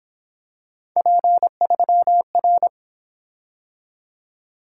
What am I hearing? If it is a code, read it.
Morse code, heard as a keyed tone, about 26 wpm, 700 Hz. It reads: P3R
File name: Speed-Racing - Three Letter-Number Combinations - 22wpm